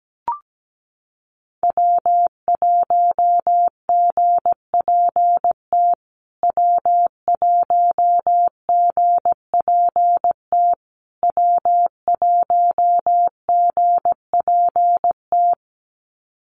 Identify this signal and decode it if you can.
Morse code, heard as a keyed tone, about 17 wpm, 700 Hz. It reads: W1GPT W1GPT W1GPT
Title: USA Call Signs Repeated 3x 17wpm